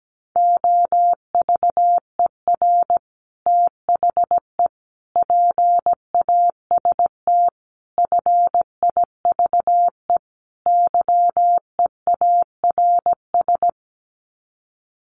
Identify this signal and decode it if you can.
Morse code, heard as a keyed tone, about 17 wpm, 700 Hz. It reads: OVER THE PAST FIVE YEARS
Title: Sets of 5 Words Encore 17wpm